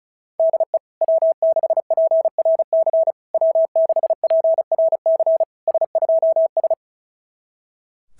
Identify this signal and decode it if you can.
Morse code, heard as a keyed tone, about 36 wpm, 660 Hz. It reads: DE W6PRC W6PRC S2S
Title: SOTA QSO Practice 35wpm